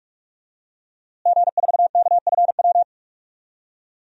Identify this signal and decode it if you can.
Morse code, heard as a keyed tone, about 41 wpm, 700 Hz. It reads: G4KFW
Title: DXCC-100 Call Signs - 5 Characters - Repeated 2x - 45wpm